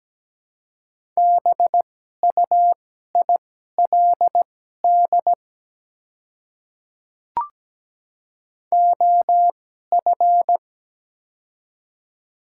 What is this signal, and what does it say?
Morse code, heard as a keyed tone, about 17 wpm, spaced as for 12 wpm, 700 Hz. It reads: BUILD OF
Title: 2x Character Spacing - Top 300 Words - 17wpm